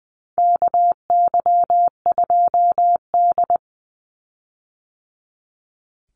Morse code, heard as a keyed tone, about 20 wpm, 700 Hz. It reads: KY2D
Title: Call Signs - 20wpm